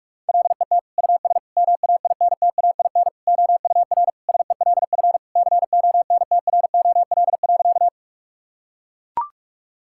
Morse code, heard as a keyed tone, about 41 wpm, 700 Hz. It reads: LET US MAINTAIN OUR SELF CONTROL.
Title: Sentences from Top 1000 Words - Review Part 5 - 45wpm